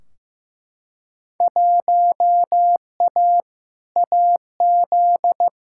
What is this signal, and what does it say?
Morse code, heard as a keyed tone, about 15 wpm, 700 Hz. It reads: 1A AZ
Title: Field Day Exchanges - 15wpm